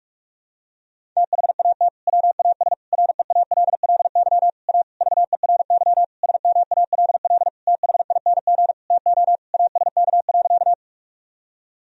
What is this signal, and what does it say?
Morse code, heard as a keyed tone, about 46 wpm, 700 Hz. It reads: THAT WAS REALLY A VERY SMALL THING TO ASK.
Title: Sentences from Top 1000 Words - Review Part 1 - 45wpm